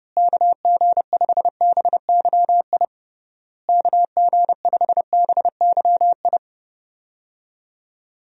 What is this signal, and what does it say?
Morse code, heard as a keyed tone, about 30 wpm, 700 Hz. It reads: KG5BYI KG5BYI
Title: USA Call Signs - 6 Characters - Repeated 3x - 30wpm